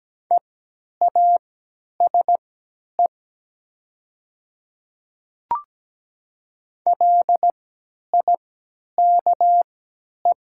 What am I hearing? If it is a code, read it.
Morse code, heard as a keyed tone, about 17 wpm, spaced as for 10 wpm, 700 Hz. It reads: EASE LIKE
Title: 3x Character Spacing - Top 500 Words - 17wpm